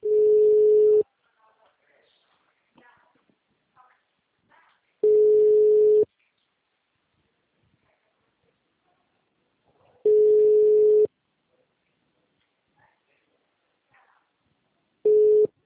Ringback tone (Europe, 425 Hz).